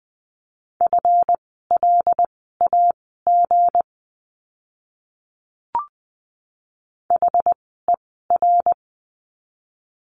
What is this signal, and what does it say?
Morse code, heard as a keyed tone, about 20 wpm, spaced as for 15 wpm, 700 Hz. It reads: FLAG HER
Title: 160 Lesson 26f - Letter g - Words 20wpm